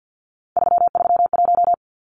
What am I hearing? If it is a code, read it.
Morse code, heard as a keyed tone, about 50 wpm, 700 Hz. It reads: 331
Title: Numbers - 3-Digit 50wpm